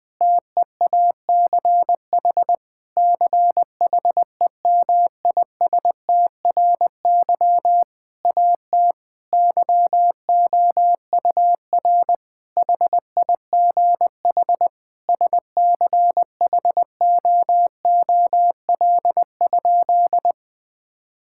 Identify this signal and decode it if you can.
Morse code, heard as a keyed tone, about 21 wpm, 700 Hz. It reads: TEACH CHEMISTRY AT YOUR HIGH SCHOOL?
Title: Sentences from Top 1000 Words - Review Part 6 - 20wpm